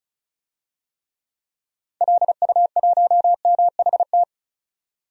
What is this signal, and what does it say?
Morse code, heard as a keyed tone, about 35 wpm, 700 Hz. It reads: LU1MHT
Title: DXCC-100 Call Signs - 6 Characters - 35wpm